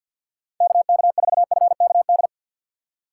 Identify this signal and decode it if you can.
Morse code, heard as a keyed tone, about 48 wpm, 700 Hz. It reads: KK4FKD
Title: USA Call Signs - 6 Characters - 4x Character Spacing - 50wpm